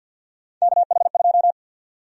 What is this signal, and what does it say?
Morse code, heard as a keyed tone, about 48 wpm, 700 Hz. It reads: KH2
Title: USA Call Sign Prefixes - 50wpm